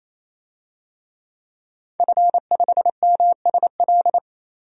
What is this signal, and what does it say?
Morse code, heard as a keyed tone, about 28 wpm, 700 Hz. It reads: F5MSL
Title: DXCC-100 Call Signs - 5 Characters - 28wpm